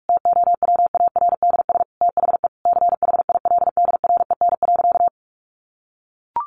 Morse code, heard as a keyed tone, about 44 wpm, 700 Hz. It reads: TOWARDS THE CHILDREN.
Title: Sentences from Top 1100 Words 45wpm